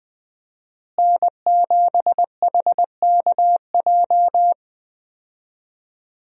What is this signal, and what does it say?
Morse code, heard as a keyed tone, about 20 wpm, 700 Hz. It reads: N7HKJ
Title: USA Call Signs - 5 Characters - Repeated 3x - 20wpm